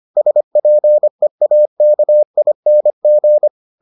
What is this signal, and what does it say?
Morse code, heard as a keyed tone, about 25 wpm, 600 Hz. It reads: SPEAKING